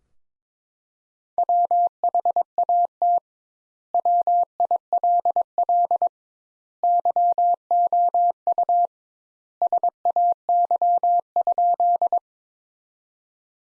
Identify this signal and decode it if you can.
Morse code, heard as a keyed tone, about 22 wpm, 700 Hz. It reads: WHAT WILL YOU SAY?
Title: Sentences Top 200 Words - 22wpm